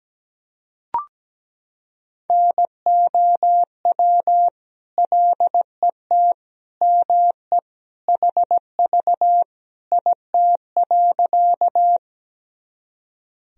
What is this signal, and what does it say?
Morse code, heard as a keyed tone, about 17 wpm, 700 Hz. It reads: NOW LET ME HV IT.
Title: Sentences from Top 300 Words and Abbreviations 17wpm